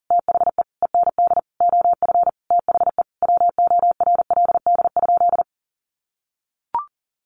Morse code, heard as a keyed tone, about 40 wpm, 700 Hz. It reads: THE END OF THE WORLD?
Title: Sentences from Top 200 Words 40wpm